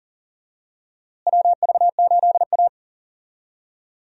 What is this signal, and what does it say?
Morse code, heard as a keyed tone, about 40 wpm, 700 Hz. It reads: WV8A
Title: USA Call Signs - 4 Characters - Repeated 3x - 40wpm